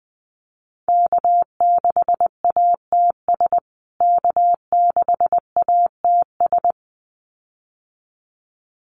Morse code, heard as a keyed tone, about 20 wpm, 700 Hz. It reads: K6ATS K6ATS
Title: USA Call Signs Repeated 2x 20wpm